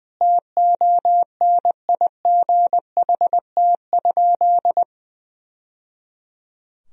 Morse code, heard as a keyed tone, about 21 wpm, 700 Hz. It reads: TONIGHT?
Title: Sentences from Top 1000 Words - Review Part 2 - 20wpm